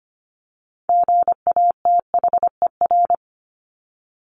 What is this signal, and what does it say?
Morse code, heard as a keyed tone, about 25 wpm, 700 Hz. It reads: GATHER
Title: Top 1501-1750 Words 25wpm